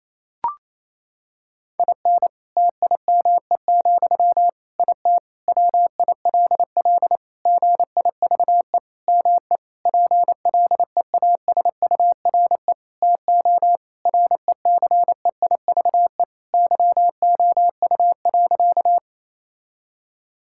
Morse code, heard as a keyed tone, about 28 wpm, 700 Hz. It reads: IN TIME, IT WILL GIVE ME PLEASURE TO RECEIVE YOU.
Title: Sentences from Top 1600 Words 28wpm